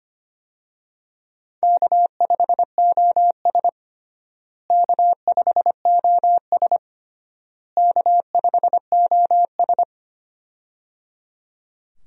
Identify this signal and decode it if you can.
Morse code, heard as a keyed tone, about 25 wpm, 700 Hz. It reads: K5OS K5OS K5OS
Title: USA Call Signs - 4 Characters - Repeated 3x - 25wpm